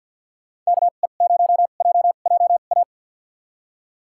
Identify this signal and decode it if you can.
Morse code, heard as a keyed tone, about 52 wpm, spaced as for 37 wpm, 700 Hz. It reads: KE0JJA
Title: USA Call Signs - 2x Character Spacing - 50wpm